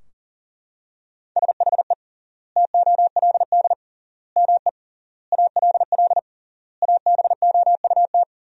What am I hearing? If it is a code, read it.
Morse code, heard as a keyed tone, about 40 wpm, 700 Hz. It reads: SHE TOLD ME ALL ABOUT
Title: 3x Word-Spacing - Sentences from Top 400 Words - 40wpm